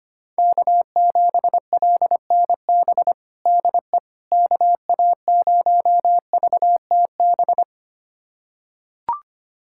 Morse code, heard as a keyed tone, about 25 wpm, 700 Hz. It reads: K7LNB DE KA0VTB
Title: USA Callsign DE Callsign 25wpm